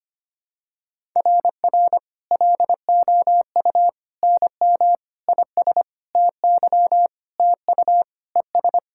Morse code, heard as a keyed tone, about 25 wpm, 700 Hz. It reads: RR LOU NM IS TY TU ES
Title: SOTA QSO Practice 25wpm